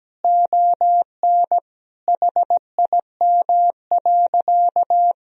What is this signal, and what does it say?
Morse code, heard as a keyed tone, about 17 wpm, 700 Hz. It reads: ON HIM.
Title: Sentences from Top 1000 Words - Review Part 1 - 17wpm